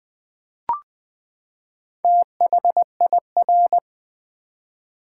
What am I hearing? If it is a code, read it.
Morse code, heard as a keyed tone, about 20 wpm, 700 Hz. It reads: THIR
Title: Build A Word - 4 Letters At A Time - 8-Letter Words - 20wpm